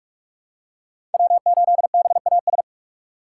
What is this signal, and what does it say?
Morse code, heard as a keyed tone, about 41 wpm, 700 Hz. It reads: W8BAS
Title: USA Call Signs - 5 Characters - 2x Character Spacing - 45wpm